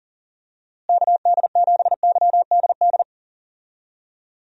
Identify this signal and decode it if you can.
Morse code, heard as a keyed tone, about 37 wpm, 700 Hz. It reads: KD7YDD
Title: USA Call Signs - 6 Characters - Repeated 2x - 40wpm